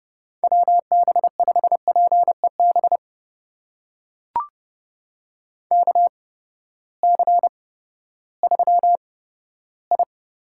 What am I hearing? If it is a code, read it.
Morse code, heard as a keyed tone, about 30 wpm, spaced as for 15 wpm, 700 Hz. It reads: WB5PEB KC3I
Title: USA Call Signs - 6 Characters - 8x Character Spacing - 30wpm